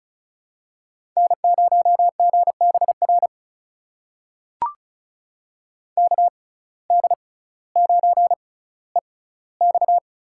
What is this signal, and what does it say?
Morse code, heard as a keyed tone, about 34 wpm, spaced as for 18 wpm, 700 Hz. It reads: N0GBR KD9EX
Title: USA Call Signs - 6x Character Spacing - 35wpm